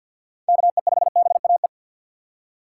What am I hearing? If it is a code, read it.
Morse code, heard as a keyed tone, about 51 wpm, 700 Hz. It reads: KE5BAE
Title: USA Call Signs - 6 Characters - 3x Character Spacing - 50wpm